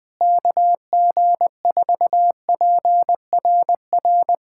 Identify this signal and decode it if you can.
Morse code, heard as a keyed tone, about 20 wpm, 700 Hz. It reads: KG4PRR
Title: USA Callsign DE Callsign - 6 Characters - 20wpm